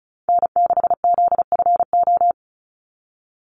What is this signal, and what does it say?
Morse code, heard as a keyed tone, about 34 wpm, 700 Hz. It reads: N6ZFO
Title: Call Signs Repeated 2x - 35wpm